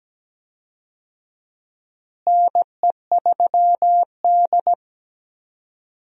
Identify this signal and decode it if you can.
Morse code, heard as a keyed tone, about 17 wpm, 700 Hz. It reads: NE3D
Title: USA Call Signs - 4 Characters - Repeated 2x - 17wpm